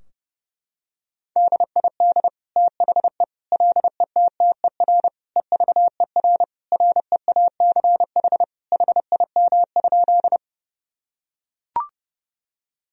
Morse code, heard as a keyed tone, about 32 wpm, 700 Hz. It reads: DID THE LETTER EVER REACH HIM?
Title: Sentences from Top 1000 Words - Review Part 2 - 30wpm